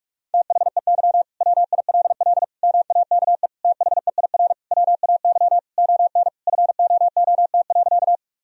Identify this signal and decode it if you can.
Morse code, heard as a keyed tone, about 47 wpm, 700 Hz. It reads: THEY WILL MAKE THEIR WAY ON FOOT.
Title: Sentences from Top 1000 Words - Review Part 3 - 45wpm